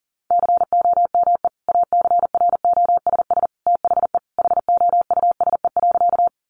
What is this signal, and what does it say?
Morse code, heard as a keyed tone, about 38 wpm, 700 Hz. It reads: COME ACROSS THE HOUSE.
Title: Sentences from Top 700 Words 40wpm